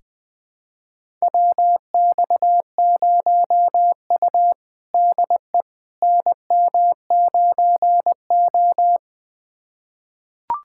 Morse code, heard as a keyed tone, about 20 wpm, 700 Hz. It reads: WX0U DE NM9O
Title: USA Callsign DE Callsign - 4 Characters - 20wpm